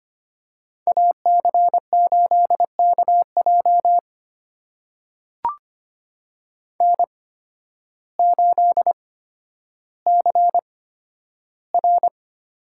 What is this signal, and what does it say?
Morse code, heard as a keyed tone, about 25 wpm, spaced as for 12 wpm, 700 Hz. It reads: AC8KJ N8CR
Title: USA Call Signs - 8x Character Spacing - 25wpm